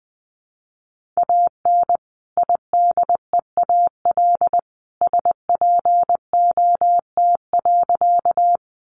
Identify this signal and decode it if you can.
Morse code, heard as a keyed tone, about 20 wpm, 700 Hz. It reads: AN IDEAL SPOT.
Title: Sentences from Top 1400 Words 20wpm